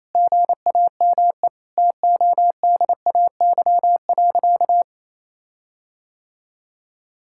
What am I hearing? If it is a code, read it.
Morse code, heard as a keyed tone, about 28 wpm, 700 Hz. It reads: GAME TODAY.
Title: Sentences from Top 1600 Words 28wpm